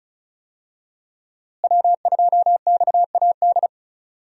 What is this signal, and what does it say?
Morse code, heard as a keyed tone, about 36 wpm, 700 Hz. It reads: W2XAD
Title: USA Call Signs - 5 Characters - Repeated 3x - 35wpm